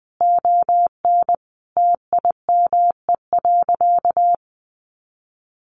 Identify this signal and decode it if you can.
Morse code, heard as a keyed tone, about 20 wpm, 700 Hz. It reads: ON TIME.
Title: A Sound Of Thunder 20wpm